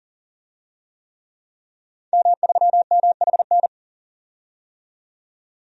Morse code, heard as a keyed tone, about 40 wpm, 700 Hz. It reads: M3MHN
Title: DXCC-100 Call Signs - 5 Characters - 40wpm